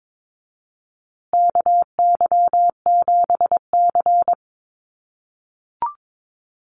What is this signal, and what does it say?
Morse code, heard as a keyed tone, about 22 wpm, 700 Hz. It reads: KY7C
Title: USA Call Signs Repeated 3x 22wpm